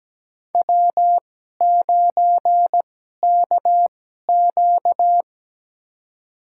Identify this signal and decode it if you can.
Morse code, heard as a keyed tone, about 17 wpm, spaced as for 12 wpm, 700 Hz. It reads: W9KQ
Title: USA Call Signs - 2x Character Spacing - 17wpm